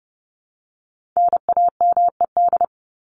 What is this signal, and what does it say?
Morse code, heard as a keyed tone, about 30 wpm, 700 Hz. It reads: NAMED
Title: ED Suffix - Top 100 Words with Base Words 30wpm